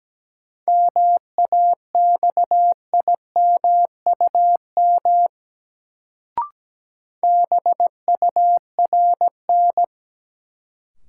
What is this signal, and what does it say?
Morse code, heard as a keyed tone, about 17 wpm, 700 Hz. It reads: MAXIMUM BURN
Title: Sentences from Top 1300 Words - New Word Drill - 17wpm